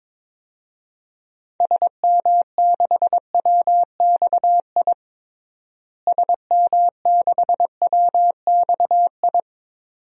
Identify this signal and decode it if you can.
Morse code, heard as a keyed tone, about 22 wpm, 700 Hz. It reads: SM6WXI SM6WXI
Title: DXCC-100 Call Signs - 6 Characters - Repeated 2x - 22wpm